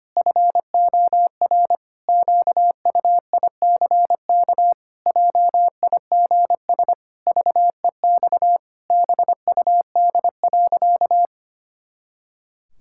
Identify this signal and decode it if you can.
Morse code, heard as a keyed tone, about 25 wpm, 700 Hz. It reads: FOR QUICK JIGS VEX BUD.
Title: Pangram Challenge v2 - 25wpm